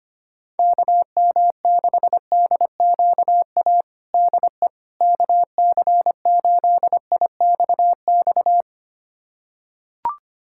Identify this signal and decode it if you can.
Morse code, heard as a keyed tone, about 25 wpm, 700 Hz. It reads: KM6DQA DE KC8IXX
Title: USA Callsign DE Callsign 25wpm